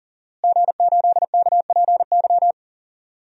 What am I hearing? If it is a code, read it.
Morse code, heard as a keyed tone, about 40 wpm, 690 Hz. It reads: G8KPY
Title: DXCC-100 Call Signs - 5 Characters - Repeated 2x - 40wpm